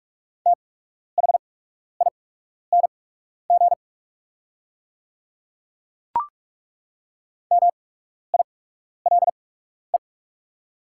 Morse code, heard as a keyed tone, about 43 wpm, spaced as for 12 wpm, 690 Hz. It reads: THING MILE
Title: 8x Character Spacing - Top 300 Words - 45wpm